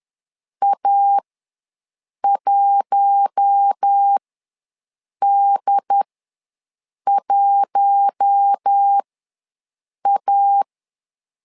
Morse code, heard as a keyed tone, about 10 wpm, spaced as for 6 wpm, 800 Hz. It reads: A1D1A
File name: LEZIONI / 4 Q1 ws 10wpm 2538